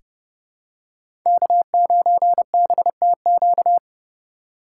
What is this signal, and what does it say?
Morse code, heard as a keyed tone, about 30 wpm, 700 Hz. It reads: K9BTQ
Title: K1USN SST QSO Practice - 30wpm